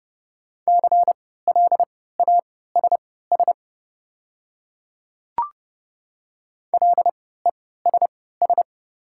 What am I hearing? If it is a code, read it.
Morse code, heard as a keyed tone, about 30 wpm, spaced as for 17 wpm, 700 Hz. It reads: CLASS LESS
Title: 3x Character Spacing - Top 500 Words - 30wpm